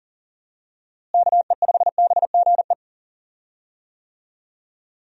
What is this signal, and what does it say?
Morse code, heard as a keyed tone, about 40 wpm, 700 Hz. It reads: KE5BGE
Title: USA Call Signs - 6 Characters - Repeated 2x - 40wpm